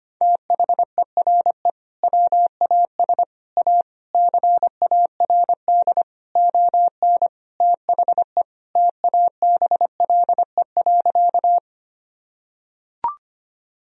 Morse code, heard as a keyed tone, about 25 wpm, 700 Hz. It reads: THERE WAS A CARD ON THE TABLE.
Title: Sentences from Top 1000 Words - Review Part 4 - 25wpm